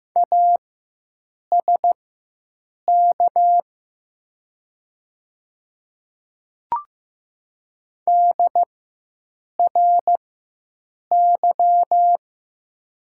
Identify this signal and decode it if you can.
Morse code, heard as a keyed tone, about 15 wpm, spaced as for 7 wpm, 700 Hz. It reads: ASK DRY
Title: 4x Character Spacing - Top 500 Words - 15wpm